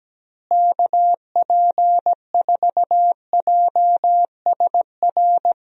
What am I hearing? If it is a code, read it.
Morse code, heard as a keyed tone, about 17 wpm, 700 Hz. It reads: KP4JSR
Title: USA Call Signs - 6 Characters - Repeated 2x - 17wpm